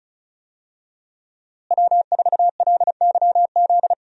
Morse code, heard as a keyed tone, about 36 wpm, 700 Hz. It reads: W4LYZ
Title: USA Call Signs - 5 Characters - Repeated 3x - 35wpm